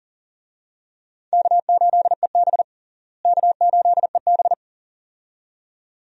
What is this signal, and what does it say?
Morse code, heard as a keyed tone, about 40 wpm, 700 Hz. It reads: K8EB K8EB
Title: USA Call Signs - 4 Characters - Repeated 2x - 40wpm